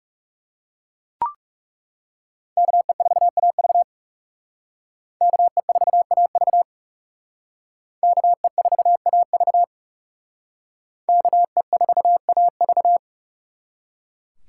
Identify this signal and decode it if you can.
Morse code, heard as a keyed tone, about 39 wpm, 700 Hz. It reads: KE4AV KE4AV KE4AV KE4AV